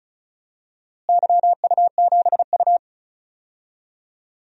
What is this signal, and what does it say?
Morse code, heard as a keyed tone, about 36 wpm, 700 Hz. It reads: YU7U
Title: DXCC-100 Call Signs - 4 Characters - Repeated 2x - 35wpm